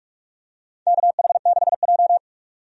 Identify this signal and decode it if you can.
Morse code, heard as a keyed tone, about 46 wpm, 700 Hz. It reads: KH6J
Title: DXCC-100 Call Signs - 4 Characters - 45wpm